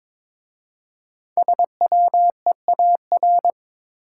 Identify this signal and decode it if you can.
Morse code, heard as a keyed tone, about 22 wpm, 700 Hz. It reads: SWEAR
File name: 5 Letter Words - First 3 Letters Given - 22wpm